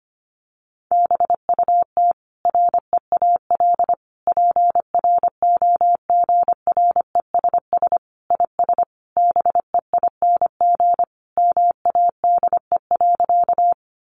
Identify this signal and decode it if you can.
Morse code, heard as a keyed tone, about 25 wpm, 700 Hz. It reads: BUT REAL PROGRESS IS BEING MADE.
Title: Sentences from Top 1400 Words 25wpm